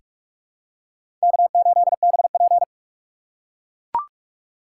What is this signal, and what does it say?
Morse code, heard as a keyed tone, about 42 wpm, 700 Hz. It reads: K8BP
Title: USA Call Signs - 4 Characters - 3x Character Spacing - 45wpm